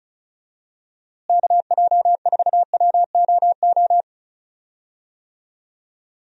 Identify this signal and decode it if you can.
Morse code, heard as a keyed tone, about 36 wpm, 700 Hz. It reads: KJ4WOO
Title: USA Call Signs - 6 Characters - Repeated 3x - 35wpm